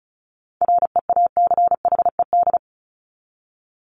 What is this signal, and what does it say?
Morse code, heard as a keyed tone, about 34 wpm, 700 Hz. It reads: REACHED
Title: ED Suffix - Top 100 Words with Base Words 35wpm